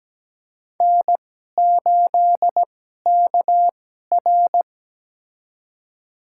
Morse code, heard as a keyed tone, about 17 wpm, spaced as for 12 wpm, 700 Hz. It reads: N8KR
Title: USA Call Signs - 4 Characters - 2x Character Spacing - 17wpm